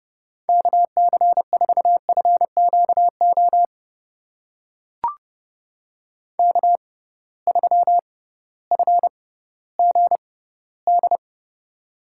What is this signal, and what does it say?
Morse code, heard as a keyed tone, about 31 wpm, spaced as for 16 wpm, 700 Hz. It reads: KC4FQO K3FGD
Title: USA Call Signs - 6x Character Spacing - 30wpm